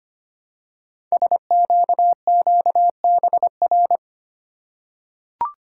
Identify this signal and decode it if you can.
Morse code, heard as a keyed tone, about 25 wpm, 700 Hz. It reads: SQQBR
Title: Code Groups - 5 Letters - 25wpm